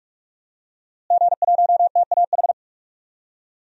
Morse code, heard as a keyed tone, about 42 wpm, 700 Hz. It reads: G1TAH
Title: DXCC-100 Call Signs - 5 Characters - Repeated 3x - 45wpm